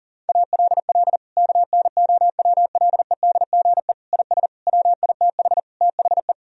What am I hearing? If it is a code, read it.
Morse code, heard as a keyed tone, about 40 wpm, 700 Hz. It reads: ALL KNOWLEDGE IS WITH THE
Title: Sentences from Top 1000 Words - Review Part 3 - 40wpm